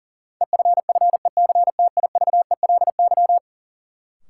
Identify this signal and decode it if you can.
Morse code, heard as a keyed tone, about 40 wpm, 700 Hz. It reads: EFFECTIVELY
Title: Sentences from Top 1400 Words - New Word Drill Alt - 40wpm